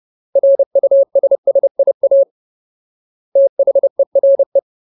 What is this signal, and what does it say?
Morse code, heard as a keyed tone, about 30 wpm, 550 Hz. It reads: RUSSIA THERE
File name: Ultimate Word List / 30WPM Ultimate